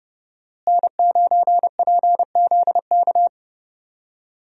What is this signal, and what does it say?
Morse code, heard as a keyed tone, about 30 wpm, 700 Hz. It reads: N9PZK
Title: USA Call Signs - 5 Characters - Repeated 3x - 30wpm